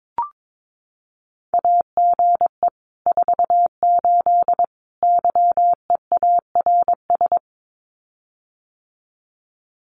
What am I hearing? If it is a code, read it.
Morse code, heard as a keyed tone, about 22 wpm, 700 Hz. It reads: AGE 48 YEARS
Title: Common QSO Phrases Encore 22wpm